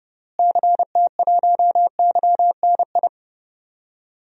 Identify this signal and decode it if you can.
Morse code, heard as a keyed tone, about 30 wpm, 700 Hz. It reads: CT1YNI
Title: DXCC-100 Call Signs - 6 Characters - Repeated 2x - 30wpm